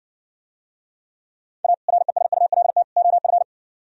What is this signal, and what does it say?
Morse code, heard as a keyed tone, about 59 wpm, 700 Hz. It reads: A RESULT OF